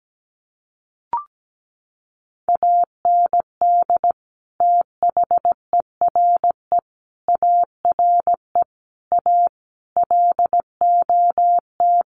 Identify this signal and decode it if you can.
Morse code, heard as a keyed tone, about 17 wpm, 700 Hz. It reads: AND THERE ARE A LOT
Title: Sets of 5 Words Encore2 17wpm